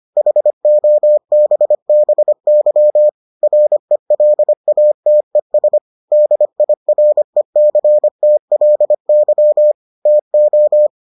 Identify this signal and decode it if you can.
Morse code, heard as a keyed tone, about 25 wpm, 600 Hz. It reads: HOBBY RELATES DIRECTLY TO